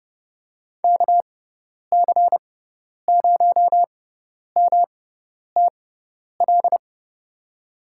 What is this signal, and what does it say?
Morse code, heard as a keyed tone, about 30 wpm, spaced as for 10 wpm, 700 Hz. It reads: KC0MTL
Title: USA Call Signs - 6 Characters - 6x Character Spacing - 30wpm